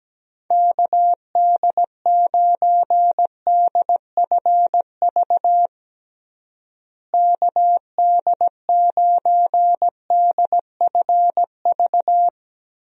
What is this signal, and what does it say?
Morse code, heard as a keyed tone, about 17 wpm, 700 Hz. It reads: KD9DFV KD9DFV
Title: USA Call Signs - 6 Characters - Repeated 3x - 17wpm